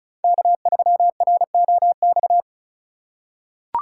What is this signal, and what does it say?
Morse code, heard as a keyed tone, about 36 wpm, 700 Hz. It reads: K3ROX
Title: USA Call Signs - 5 Characters - 4x Character Spacing - 35wpm